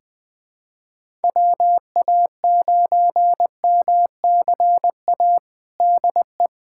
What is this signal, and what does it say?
Morse code, heard as a keyed tone, about 20 wpm, 700 Hz. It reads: WA9MCA DE
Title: USA Callsign DE Callsign - 6 Characters - 20wpm